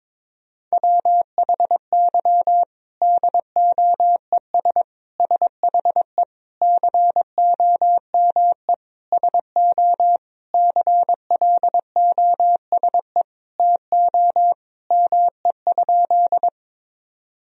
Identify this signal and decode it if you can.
Morse code, heard as a keyed tone, about 22 wpm, 700 Hz. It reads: WHY DOES SHE COME SO CLOSE TO ME?
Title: Sentences from Top 1000 Words - Review Part 1 - 22wpm